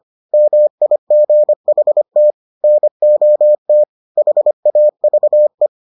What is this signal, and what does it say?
Morse code, heard as a keyed tone, about 25 wpm, 600 Hz. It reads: MIGHT NOT HAVE